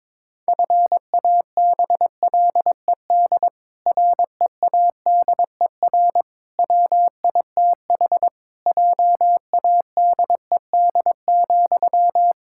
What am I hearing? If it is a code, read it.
Morse code, heard as a keyed tone, about 22 wpm, 700 Hz. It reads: FABLED READER WITH JADED,
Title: Pangram Challenge v2 - 22wpm